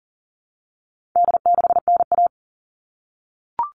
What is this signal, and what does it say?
Morse code, heard as a keyed tone, about 40 wpm, 700 Hz. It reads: D6NA
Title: All Cut Numbers - 4-Digit 40wpm